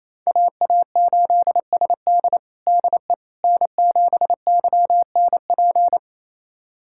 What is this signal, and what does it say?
Morse code, heard as a keyed tone, about 28 wpm, 700 Hz. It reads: AA8SD DE N7YNP
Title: USA Callsign DE Callsign - 5 Characters - 28wpm